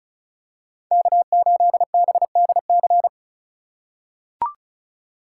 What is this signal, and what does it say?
Morse code, heard as a keyed tone, about 36 wpm, 700 Hz. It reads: K8BDC
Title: USA Call Signs - 5 Characters - 3x Character Spacing - 35wpm